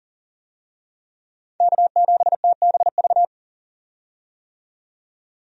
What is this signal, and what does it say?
Morse code, heard as a keyed tone, about 40 wpm, 700 Hz. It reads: K7TBV
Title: USA Call Signs - 5 Characters - 40wpm